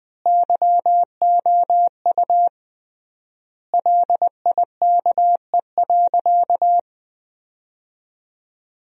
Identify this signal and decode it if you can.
Morse code, heard as a keyed tone, about 20 wpm, 700 Hz. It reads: YOU LIKE.
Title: 3x Word-Spacing - Sentences from Top 100 Words - 20wpm